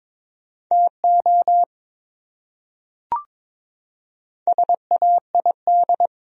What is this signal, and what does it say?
Morse code, heard as a keyed tone, about 23 wpm, 700 Hz. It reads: TO SAID
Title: Top 100 Words Repeated 22wpm